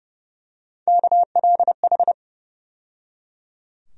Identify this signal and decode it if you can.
Morse code, heard as a keyed tone, about 31 wpm, 700 Hz. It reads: KLH
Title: USA Call Sign Postfixes - 3 Letters 30wpm